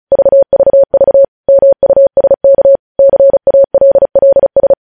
Morse code, heard as a keyed tone, about 36 wpm, 550 Hz. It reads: VVV MUSK CALLS
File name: podcast / News headlines-1x-35x35